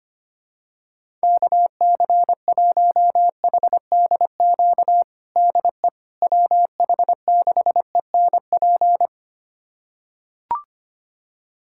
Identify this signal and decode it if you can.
Morse code, heard as a keyed tone, about 25 wpm, 700 Hz. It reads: KC1HDQ DE WH6ENP
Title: USA Callsign DE Callsign - 6 Characters - 25wpm